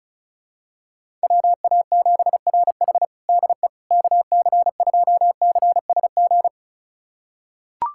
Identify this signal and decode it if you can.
Morse code, heard as a keyed tone, about 34 wpm, 700 Hz. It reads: WA7RH DE KC2CSG
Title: USA Callsign DE Callsign 35wpm